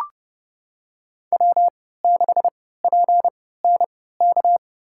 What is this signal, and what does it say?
Morse code, heard as a keyed tone, about 30 wpm, spaced as for 17 wpm, 700 Hz. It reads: W6PNK
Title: USA Call Signs - 3x Character Spacing - 30wpm